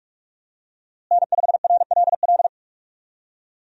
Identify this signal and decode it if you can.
Morse code, heard as a keyed tone, about 44 wpm, 700 Hz. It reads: N5RLL
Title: USA Call Signs - 5 Characters - Repeated 2x - 45wpm